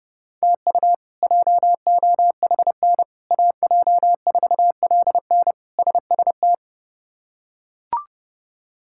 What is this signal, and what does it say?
Morse code, heard as a keyed tone, about 30 wpm, 700 Hz. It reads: TU JOHN AJ4LN SST
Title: K1USN SST QSO Practice - 30wpm